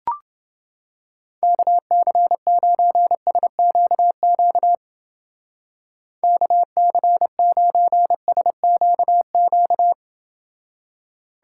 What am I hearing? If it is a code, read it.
Morse code, heard as a keyed tone, about 29 wpm, 700 Hz. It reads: KC9SQQ KC9SQQ